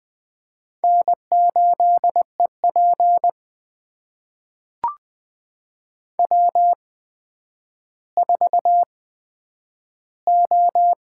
Morse code, heard as a keyed tone, about 20 wpm, spaced as for 10 wpm, 700 Hz. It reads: N8EP W4O
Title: USA Call Signs - 8x Character Spacing - 20wpm